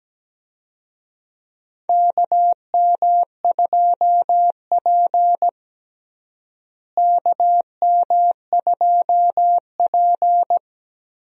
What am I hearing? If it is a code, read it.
Morse code, heard as a keyed tone, about 17 wpm, 700 Hz. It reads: KM2P KM2P
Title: USA Call Signs - 4 Characters - Repeated 3x - 17wpm